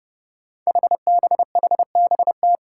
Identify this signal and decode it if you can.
Morse code, heard as a keyed tone, about 30 wpm, 700 Hz. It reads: HBHBT
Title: Five Challenging Letter-Number Combinations 30wpm